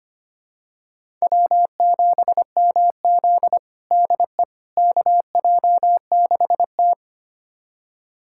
Morse code, heard as a keyed tone, about 25 wpm, 700 Hz. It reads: W7MZ DE KJ6T
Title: USA Callsign DE Callsign - 4 Characters - 25wpm